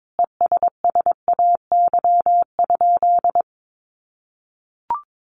Morse code, heard as a keyed tone, about 22 wpm, 700 Hz. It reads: ESSAY?
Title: Sentences from Top 1500 Words 22wpm